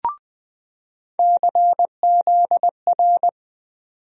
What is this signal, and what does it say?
Morse code, heard as a keyed tone, about 20 wpm, 700 Hz. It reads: CZR
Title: USA Call Sign Postfixes - 3 Letters 20wpm